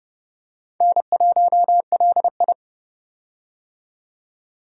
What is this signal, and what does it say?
Morse code, heard as a keyed tone, about 30 wpm, 700 Hz. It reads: N1LI
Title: USA Call Signs - 4 Characters - 30wpm